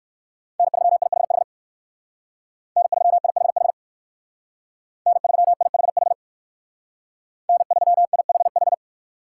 Morse code, heard as a keyed tone, about 58 wpm, 700 Hz. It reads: N3IHH N3IHH N3IHH N3IHH